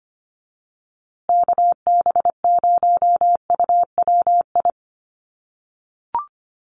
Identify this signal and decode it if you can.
Morse code, heard as a keyed tone, about 25 wpm, 700 Hz. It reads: KB0UWI
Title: USA Call Signs Repeated 2x 25wpm